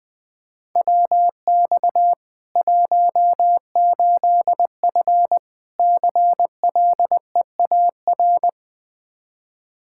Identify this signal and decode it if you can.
Morse code, heard as a keyed tone, about 20 wpm, 700 Hz. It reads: WX 18F CLEAR
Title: Practice QSO - Exchange 2 - Info - 20wpm